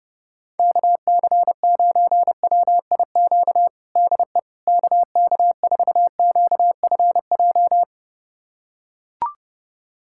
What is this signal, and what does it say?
Morse code, heard as a keyed tone, about 30 wpm, 700 Hz. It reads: KC9WIQ DE KK4QFJ
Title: USA Callsign DE Callsign - 6 Characters - 30wpm